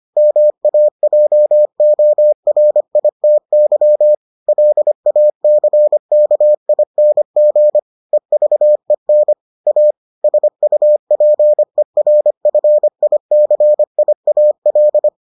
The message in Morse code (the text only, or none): MAJORITY LACKING EVEN A SUPERFICIAL